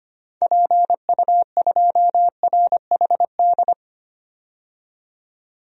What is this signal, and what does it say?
Morse code, heard as a keyed tone, about 25 wpm, 700 Hz. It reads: PU2RHD
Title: DXCC-100 Call Signs - 6 Characters - 25wpm